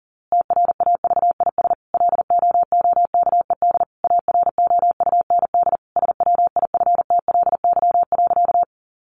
Morse code, heard as a keyed tone, about 40 wpm, 700 Hz. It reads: TRAVIS LOOKED AROUND SWIFTLY.
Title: A Sound Of Thunder 40wpm